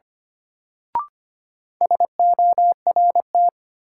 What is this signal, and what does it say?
Morse code, heard as a keyed tone, about 25 wpm, 700 Hz. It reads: SORT
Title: Sentences from Top 1000 Words - Review Part 3 - New Word Drill - 25wpm